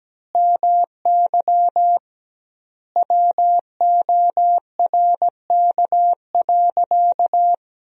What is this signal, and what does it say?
Morse code, heard as a keyed tone, about 17 wpm, 700 Hz. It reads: MY WORK.
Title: 2x Word-Spacing - Sentences from Top 300 Words - 17wpm